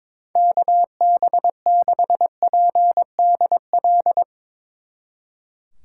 Morse code, heard as a keyed tone, about 24 wpm, 700 Hz. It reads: KB6PDL
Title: Speed Racing - USA Call Signs 22wpm